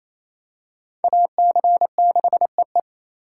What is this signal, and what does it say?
Morse code, heard as a keyed tone, about 28 wpm, 700 Hz. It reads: AC6EE
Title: USA Call Signs - 5 Characters - 4x Character Spacing - 28wpm